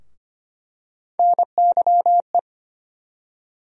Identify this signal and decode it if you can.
Morse code, heard as a keyed tone, about 25 wpm, 700 Hz. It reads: NYE
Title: USA Call Sign Postfixes - 3 Letters 25wpm